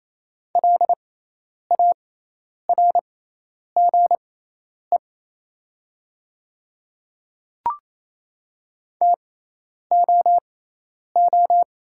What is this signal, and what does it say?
Morse code, heard as a keyed tone, about 28 wpm, spaced as for 10 wpm, 690 Hz. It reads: LARGE TOO
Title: 6x Character Spacing - Top 200 Words - 28wpm